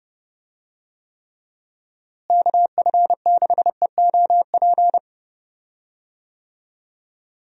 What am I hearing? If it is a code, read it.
Morse code, heard as a keyed tone, about 30 wpm, 700 Hz. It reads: KF6EOP
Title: USA Call Signs - 6 Characters - Repeated 3x - 30wpm